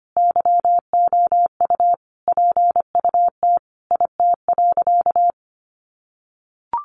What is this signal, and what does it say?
Morse code, heard as a keyed tone, about 25 wpm, 700 Hz. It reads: YOU PUT IT.
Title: Sentences from Top 200 Words 25wpm